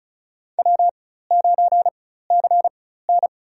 Morse code, heard as a keyed tone, about 36 wpm, spaced as for 16 wpm, 700 Hz. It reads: W9CN
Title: USA Call Signs - 4 Characters - 4x Character Spacing - 35wpm